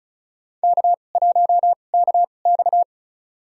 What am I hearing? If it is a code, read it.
Morse code, heard as a keyed tone, about 36 wpm, spaced as for 25 wpm, 700 Hz. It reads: K1KX
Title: USA Call Signs - 4 Characters - 2x Character Spacing - 35wpm